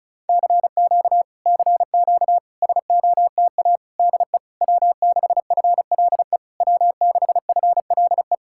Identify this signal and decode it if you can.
Morse code, heard as a keyed tone, about 39 wpm, 700 Hz. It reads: CQ CQ SOTA DE W6FLE W6FLE